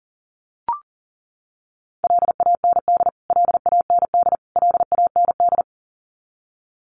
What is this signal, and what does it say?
Morse code, heard as a keyed tone, about 40 wpm, 700 Hz. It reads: LAND LAND LAND
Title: Top 200 Words Repeated 40wpm